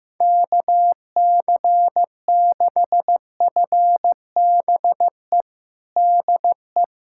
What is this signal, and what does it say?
Morse code, heard as a keyed tone, about 15 wpm, 700 Hz. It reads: KC6FBE DE
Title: USA Callsign DE Callsign - 6 Characters - 15wpm